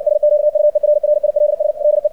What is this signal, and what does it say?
Morse code, heard as a keyed tone, about 60 wpm, 600 Hz. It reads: SQUERRILS DE
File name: archived / CRSnet